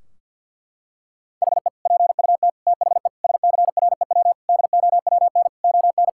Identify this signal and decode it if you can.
Morse code, heard as a keyed tone, about 50 wpm, 700 Hz. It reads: HE PUT THE SCREW DOWN ON
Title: Sentences from Top 1600 Words 50wpm